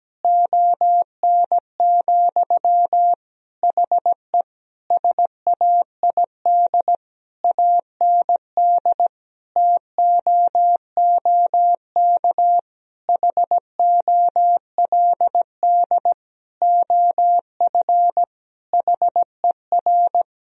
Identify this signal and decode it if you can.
Morse code, heard as a keyed tone, about 17 wpm, 700 Hz. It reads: ON, HE SAID AND TOOK HOLD OF HER
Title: Sentences from Top 1000 Words - Review Part 2 - 17wpm